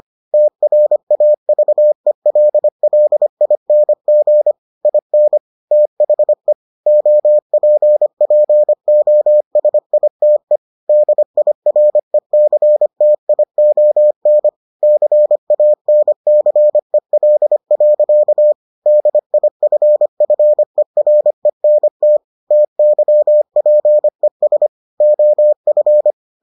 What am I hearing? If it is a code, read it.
Morse code, heard as a keyed tone, about 25 wpm, 600 Hz. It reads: TRAVELLING IN THE OPPOSITE DIRECTION CANCEL. DIFFERENT TYPES OF